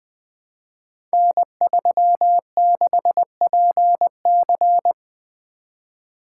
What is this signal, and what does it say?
Morse code, heard as a keyed tone, about 20 wpm, 700 Hz. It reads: N36PC
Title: Code Groups - 5 Letter:Numbers - 20wpm